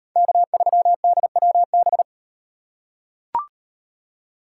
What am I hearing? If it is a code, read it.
Morse code, heard as a keyed tone, about 38 wpm, 700 Hz. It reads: K3DWB